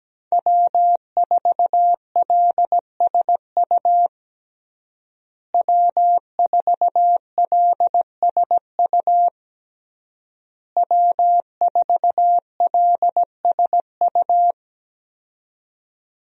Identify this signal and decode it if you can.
Morse code, heard as a keyed tone, about 17 wpm, 700 Hz. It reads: W4LSU W4LSU W4LSU
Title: USA Call Signs - 5 Characters - Repeated 3x - 17wpm